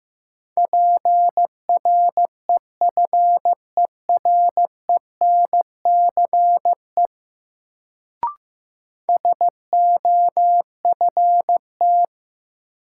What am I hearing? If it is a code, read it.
Morse code, heard as a keyed tone, about 15 wpm, 700 Hz. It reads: PREFERENCE SOFT
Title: Sentences from Top 1500 Words - New Word Drill Alt - 15wpm